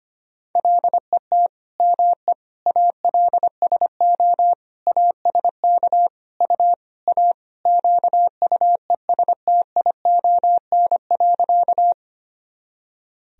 Morse code, heard as a keyed tone, about 25 wpm, 700 Hz. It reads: LET ME ALSO ASK U A QUESTION.
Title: Sentences from Top 400 Words and Abbreviations 25wpm